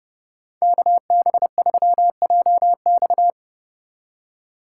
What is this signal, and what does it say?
Morse code, heard as a keyed tone, about 31 wpm, 700 Hz. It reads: KB3JX